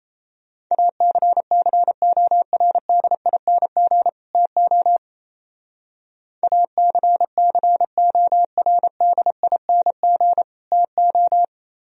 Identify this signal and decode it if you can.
Morse code, heard as a keyed tone, about 34 wpm, 700 Hz. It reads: ACCORDING TO ACCORDING TO
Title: Speed-Racing - Sets of 2 Words 28wpm